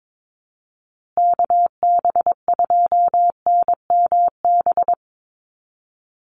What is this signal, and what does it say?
Morse code, heard as a keyed tone, about 22 wpm, 700 Hz. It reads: KB2NMB
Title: USA Call Signs Repeated 2x 22wpm